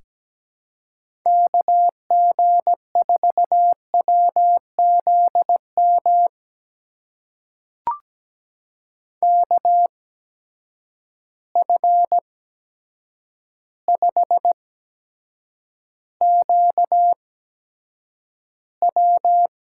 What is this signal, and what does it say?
Morse code, heard as a keyed tone, about 17 wpm, spaced as for 8 wpm, 700 Hz. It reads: KG4WZM KF5QW
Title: USA Call Signs - 6 Characters - 8x Character Spacing - 17wpm